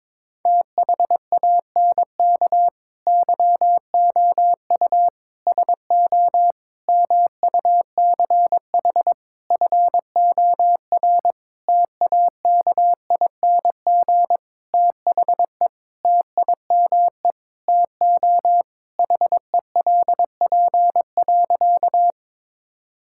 Morse code, heard as a keyed tone, about 22 wpm, 700 Hz. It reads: THANK YOU SO MUCH FOR TAKING THE TIME TO HELP.
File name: Sentences with 9-Word Phrases - 22wpm